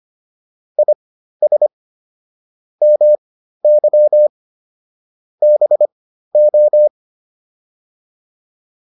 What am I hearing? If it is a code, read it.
Morse code, heard as a keyed tone, about 25 wpm, spaced as for 13 wpm, 600 Hz. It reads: IS MY BO